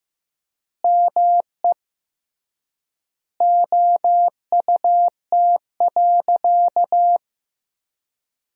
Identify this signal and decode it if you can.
Morse code, heard as a keyed tone, about 15 wpm, 700 Hz. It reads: ME OUT.
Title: 3x Word-Spacing - Sentences from Top 300 Words - 15wpm